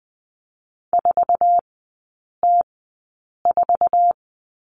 Morse code, heard as a keyed tone, about 20 wpm, spaced as for 8 wpm, 700 Hz. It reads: 4T4
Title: 053 Lesson 10e - Letter r - Three Characters 20wpm